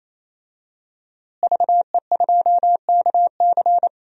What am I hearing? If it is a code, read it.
Morse code, heard as a keyed tone, about 28 wpm, 700 Hz. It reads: VE2KC
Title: DXCC-100 Call Signs - 5 Characters - Repeated 3x - 28wpm